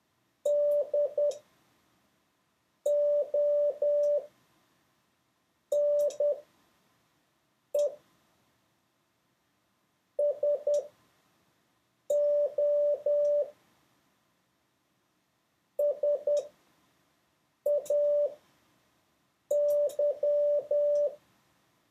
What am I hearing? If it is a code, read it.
Morse code, heard as a keyed tone, about 10 wpm, spaced as for 5 wpm, 580 Hz. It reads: DONE SO SAY
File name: 6of6-test